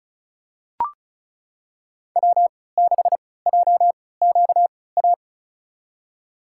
Morse code, heard as a keyed tone, about 36 wpm, spaced as for 20 wpm, 700 Hz. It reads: W6JQA
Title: USA Call Signs - 5 Characters - 3x Character Spacing - 35wpm